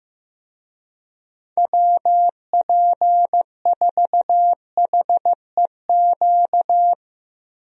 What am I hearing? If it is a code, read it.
Morse code, heard as a keyed tone, about 15 wpm, 700 Hz. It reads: WP4HEQ
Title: USA Call Signs - 6 Characters - Repeated 2x - 15wpm